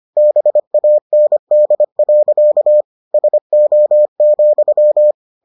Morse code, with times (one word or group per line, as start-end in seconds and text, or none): 0.2-2.8 BAND.
3.1-5.1 SO,